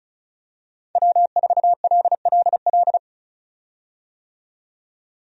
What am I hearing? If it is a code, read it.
Morse code, heard as a keyed tone, about 36 wpm, 700 Hz. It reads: W4LLL
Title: USA Call Signs - 5 Characters - 35wpm